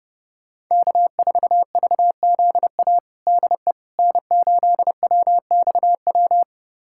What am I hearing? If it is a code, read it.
Morse code, heard as a keyed tone, about 30 wpm, 700 Hz. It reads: K4VZA DE N8WXW
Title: USA Callsign DE Callsign - 5 Characters - 30wpm